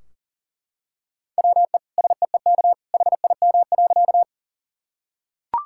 Morse code, heard as a keyed tone, about 40 wpm, 700 Hz. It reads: WE SEEK HIM.
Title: Sentences from Top 1600 Words 40wpm